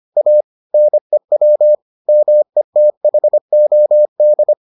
Morse code, heard as a keyed tone, about 25 wpm, 600 Hz. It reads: A NEW METHOD